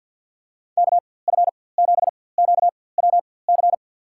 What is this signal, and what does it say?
Morse code, heard as a keyed tone, about 52 wpm, spaced as for 24 wpm, 700 Hz. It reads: KF7QWC
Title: USA Call Signs - 6 Characters - 4x Character Spacing - 50wpm